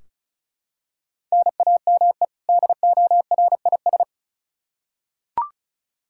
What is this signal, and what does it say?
Morse code, heard as a keyed tone, about 36 wpm, 700 Hz. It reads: NAME DORIS
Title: Practice QSO - Exchange 1 - Beginning - 35wpm